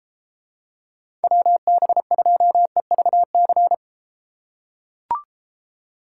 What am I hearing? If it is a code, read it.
Morse code, heard as a keyed tone, about 32 wpm, 700 Hz. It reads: WB2EVC